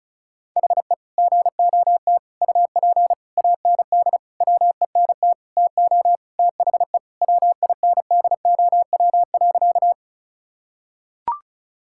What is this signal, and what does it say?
Morse code, heard as a keyed tone, about 36 wpm, 700 Hz. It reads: HE GOT UP AND WENT TO THE WINDOW.
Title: Sentences from Top 1600 Words 35wpm